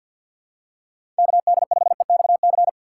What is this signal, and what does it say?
Morse code, heard as a keyed tone, about 52 wpm, 700 Hz. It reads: KD5EXC
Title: USA Call Signs - 6 Characters - 3x Character Spacing - 50wpm